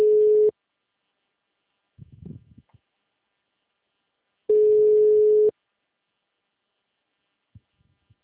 Ringback tone (Europe, 425 Hz).
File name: out- -2011- - - .56376